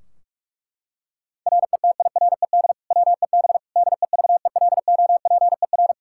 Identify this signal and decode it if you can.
Morse code, heard as a keyed tone, about 42 wpm, 700 Hz. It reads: RETIRED WEB DEVELOPER
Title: Practice QSO - Exchange 3 - More Info - 45wpm